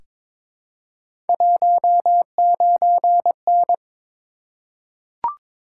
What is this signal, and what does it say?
Morse code, heard as a keyed tone, about 22 wpm, 700 Hz. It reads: 19N
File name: practice / 22wpm / Three Letter-Number Combinations - 22wpm